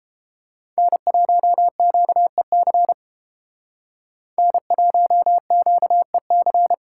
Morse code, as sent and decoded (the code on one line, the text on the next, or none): -. .---- --.- . -.-. / -. .---- --.- . -.-.
N1QEC N1QEC